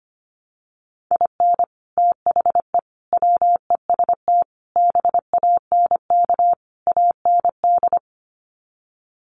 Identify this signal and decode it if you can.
Morse code, heard as a keyed tone, about 25 wpm, 700 Hz. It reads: IN THE WEST BANK AND
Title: Sets of 5 Words Encore2 25wpm